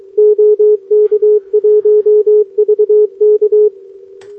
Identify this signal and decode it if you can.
Morse code, heard as a keyed tone, about 21 wpm, 420 Hz. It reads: OK1VK
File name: OK QRP 2026 / OKQRP2026